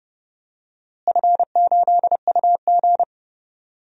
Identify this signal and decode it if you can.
Morse code, heard as a keyed tone, about 30 wpm, 700 Hz. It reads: F8UG
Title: DXCC-100 Call Signs - 4 Characters - Repeated 2x - 30wpm